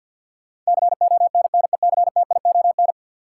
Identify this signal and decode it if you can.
Morse code, heard as a keyed tone, about 53 wpm, 700 Hz. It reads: CONNECTION
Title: Sentences from Top 1300 Words - New Word Drill - 50wpm